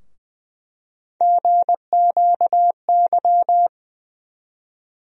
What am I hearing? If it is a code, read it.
Morse code, heard as a keyed tone, about 20 wpm, 700 Hz. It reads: GQY
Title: USA Call Sign Postfixes - 3 Letters 20wpm